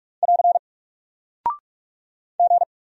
Morse code, heard as a keyed tone, about 44 wpm, 700 Hz. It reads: <AR> G
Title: Single Character (Letter:Number:Punctuation:Prosign) Alt 45wpm